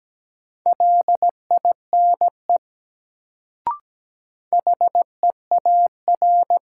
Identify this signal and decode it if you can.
Morse code, heard as a keyed tone, about 17 wpm, 700 Hz. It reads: LINE HEAR
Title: Sets of 5 Words - New Word Drill - 17wpm